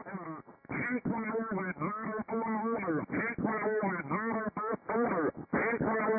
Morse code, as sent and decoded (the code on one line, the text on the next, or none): none